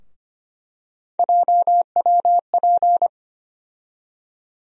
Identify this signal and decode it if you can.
Morse code, heard as a keyed tone, about 25 wpm, 700 Hz. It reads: JWP
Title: USA Call Sign Postfixes - 3 Letters 25wpm